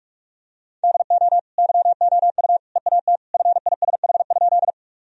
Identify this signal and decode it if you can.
Morse code, heard as a keyed tone, about 46 wpm, 700 Hz. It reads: DO YOU EAT FISH?
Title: Sentences from Top 1000 Words - Review Part 1 - 45wpm